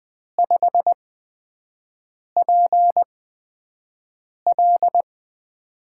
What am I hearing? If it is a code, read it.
Morse code, heard as a keyed tone, about 20 wpm, spaced as for 5 wpm, 700 Hz. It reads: 5PL
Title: USA Call Signs - 8x Character Spacing - 20wpm